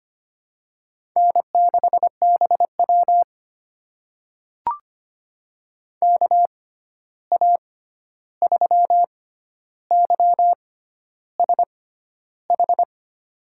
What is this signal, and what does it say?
Morse code, heard as a keyed tone, about 25 wpm, spaced as for 11 wpm, 700 Hz. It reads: N6BW KA3YSH
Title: USA Call Signs - 6x Character Spacing - 25wpm